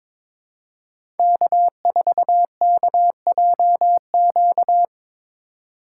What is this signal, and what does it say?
Morse code, heard as a keyed tone, about 22 wpm, 700 Hz. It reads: K4KJQ
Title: USA Call Signs - 5 Characters - 2x Character Spacing - 22wpm